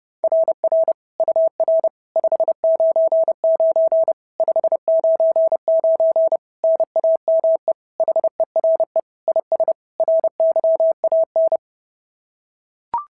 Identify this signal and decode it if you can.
Morse code, heard as a keyed tone, about 26 wpm, 650 Hz. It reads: RR UR 599 599 NAME HERE IS RYAN
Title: SOTA QSO Practice 30wpm